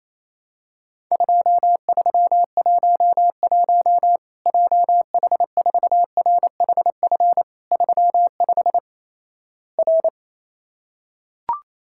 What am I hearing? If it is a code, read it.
Morse code, heard as a keyed tone, about 28 wpm, 700 Hz. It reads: 2311 JH4RHF 35 R
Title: Worked All Europe DX Contest - QTC Exchange - 28wpm